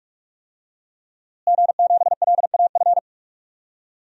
Morse code, heard as a keyed tone, about 41 wpm, 700 Hz. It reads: G7LAF
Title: DXCC-100 Call Signs - 5 Characters - Repeated 2x - 45wpm